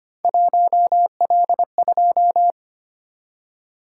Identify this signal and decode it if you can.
Morse code, heard as a keyed tone, about 25 wpm, 700 Hz. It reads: 1L2
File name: Three Letter-Number Combinations - 25wpm